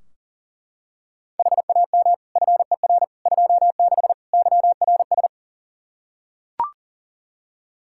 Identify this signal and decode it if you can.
Morse code, heard as a keyed tone, about 40 wpm, 700 Hz. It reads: HAM FER 26 YRS
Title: Practice QSO - Exchange 3 - More Info - 40wpm